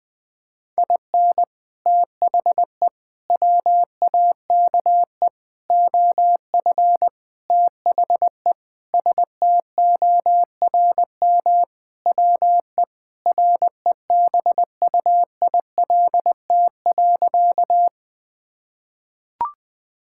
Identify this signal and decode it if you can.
Morse code, heard as a keyed tone, about 20 wpm, 700 Hz. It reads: IN THE WAKE OF THE STORM WE REBUILT.
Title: Sentences with 5-Word Phrases - Spoken Beginning - 20wpm